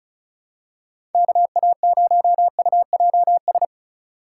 Morse code, heard as a keyed tone, about 36 wpm, 700 Hz. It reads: KA0UJS
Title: USA Call Signs - 4x Character Spacing - 35wpm